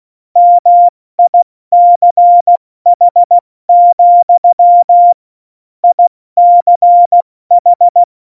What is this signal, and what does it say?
Morse code, heard as a keyed tone, about 16 wpm, 700 Hz. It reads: MICH, ICH
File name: Morsekurs / Witze / Witz 29. Dezember 2020, 70 F80